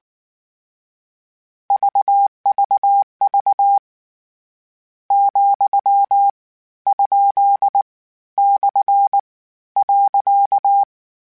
Morse code, heard as a keyed tone, about 19 wpm, 800 Hz. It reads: VVV , ? / .
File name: Misc / Misc-mp3s / Punct @18